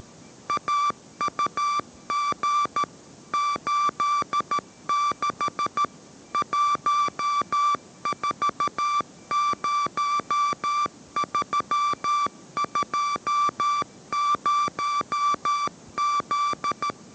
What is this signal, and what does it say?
Morse code, heard as a keyed tone, about 13 wpm, 1170 Hz. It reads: AUG 86 140320Z